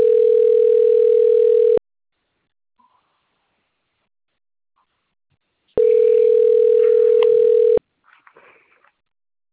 Ringback tone (North America), 440 + 480 Hz.